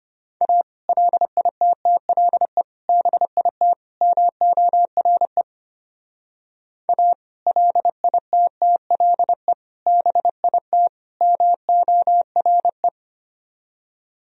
Morse code, heard as a keyed tone, about 29 wpm, 700 Hz. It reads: A LITTLE BIT MORE A LITTLE BIT MORE
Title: Speed-Racing - Sets of 4 Words 25wpm